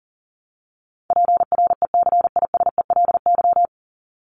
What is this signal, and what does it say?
Morse code, heard as a keyed tone, about 40 wpm, 700 Hz. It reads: PRECISELY
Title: LY Suffix - Top 100 Words with Base Words 40wpm